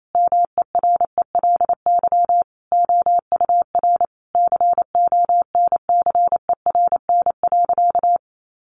Morse code, heard as a keyed tone, about 28 wpm, 700 Hz. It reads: MERELY OUR CONCERN.
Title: Sentences from Top 1400 Words 28wpm